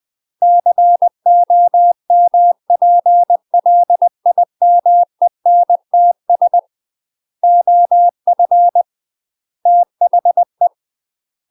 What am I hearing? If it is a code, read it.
Morse code, heard as a keyed tone, about 20 wpm, 700 Hz. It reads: COMPLIMENTS OF THE
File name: UK MASTER QSOs TRAINING / 20WPM / UK MASTER QSOs 0029